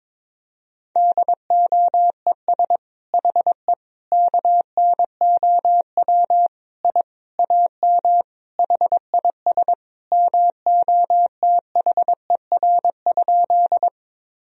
Morse code, heard as a keyed tone, about 22 wpm, 700 Hz. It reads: DOES HE KNOW I AM HIS MOTHER?
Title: Sentences from Top 1000 Words - Review Part 2 - 22wpm